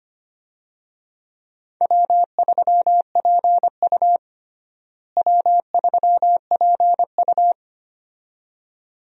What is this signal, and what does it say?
Morse code, heard as a keyed tone, about 25 wpm, 700 Hz. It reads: W3PU W3PU
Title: USA Call Signs - 4 Characters - Repeated 2x - 25wpm